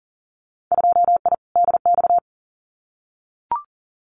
Morse code, heard as a keyed tone, about 40 wpm, 700 Hz. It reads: 2I DX
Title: Winter Field Day Exchanges 40wpm